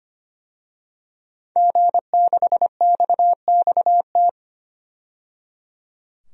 Morse code, heard as a keyed tone, about 25 wpm, 700 Hz. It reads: G6XXT
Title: DXCC-100 Call Signs - 5 Characters - 25wpm